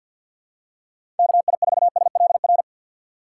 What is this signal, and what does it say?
Morse code, heard as a keyed tone, about 52 wpm, 700 Hz. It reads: KI4SLR
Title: USA Call Signs - 6x Character Spacing - 50wpm